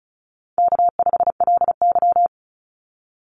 Morse code, heard as a keyed tone, about 34 wpm, 700 Hz. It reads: K5LY
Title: Call Signs Repeated 2x - 35wpm